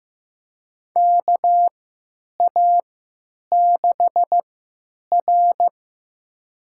Morse code, heard as a keyed tone, about 15 wpm, spaced as for 9 wpm, 700 Hz. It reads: KA6R
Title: USA Call Signs - 3x Character Spacing - 15wpm